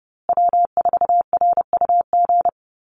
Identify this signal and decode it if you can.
Morse code, heard as a keyed tone, about 30 wpm, 700 Hz. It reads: W4RUG
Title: USA Call Signs Repeated 2x 30wpm